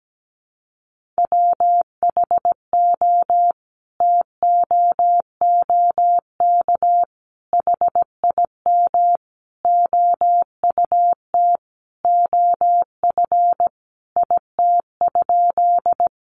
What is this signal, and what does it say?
Morse code, heard as a keyed tone, about 17 wpm, 700 Hz. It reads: WHO TOOK HIM OUT OF IT?
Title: Sentences from Top 300 Words 17wpm